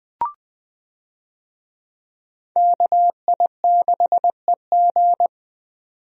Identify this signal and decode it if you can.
Morse code, heard as a keyed tone, about 20 wpm, 700 Hz. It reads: KI6EG
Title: USA Call Signs - 5 Characters - 20wpm